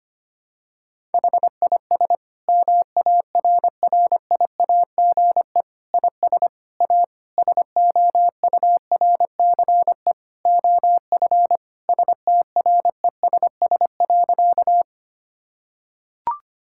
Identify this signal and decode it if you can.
Morse code, heard as a keyed tone, about 25 wpm, 700 Hz. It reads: HIS MARRIAGE IS A SOURCE OF STRESS.
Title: Sentences from Top 1000 Words - Review Part 6 - 25wpm